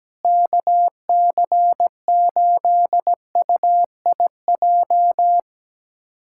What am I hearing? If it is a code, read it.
Morse code, heard as a keyed tone, about 17 wpm, 700 Hz. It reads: KC8UIJ
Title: USA Callsign DE Callsign - 6 Characters - 17wpm